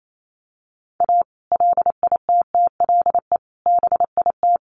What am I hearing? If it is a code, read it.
Morse code, heard as a keyed tone, about 28 wpm, 700 Hz. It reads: A LITTLE BIT
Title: Sets of 5 Words Encore2 28wpm